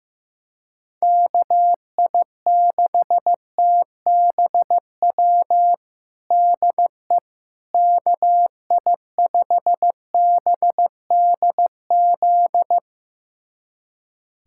Morse code, heard as a keyed tone, about 14 wpm, 700 Hz. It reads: KI6TBW DE KI5BDZ
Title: USA Callsign DE Callsign - 6 Characters - 15wpm